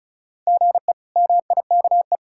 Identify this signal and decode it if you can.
Morse code, heard as a keyed tone, about 36 wpm, 700 Hz. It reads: GE MIKE
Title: K1USN SST QSO Practice - 35wpm